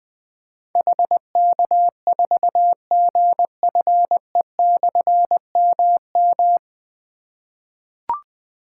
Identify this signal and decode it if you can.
Morse code, heard as a keyed tone, about 20 wpm, 700 Hz. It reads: HK4GFE/MM
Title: Difficult Call Signs 20wpm